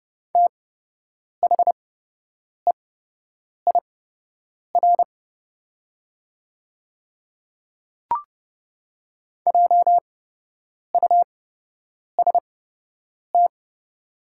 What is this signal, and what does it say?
Morse code, heard as a keyed tone, about 30 wpm, spaced as for 8 wpm, 700 Hz. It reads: THEIR JUST
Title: 8x Character Spacing - Top 200 Words - 30wpm